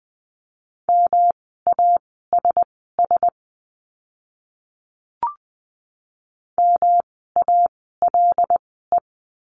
Morse code, heard as a keyed tone, about 20 wpm, spaced as for 15 wpm, 700 Hz. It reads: MASS MALE
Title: 108 Lesson 18f - Letter m - Words 20wpm